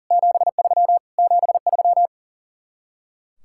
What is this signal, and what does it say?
Morse code, heard as a keyed tone, about 51 wpm, 700 Hz. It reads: 73 73